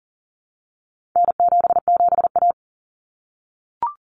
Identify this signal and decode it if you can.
Morse code, heard as a keyed tone, about 40 wpm, 700 Hz. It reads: N77A
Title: Common Cut Numbers - 4-Digit 40wpm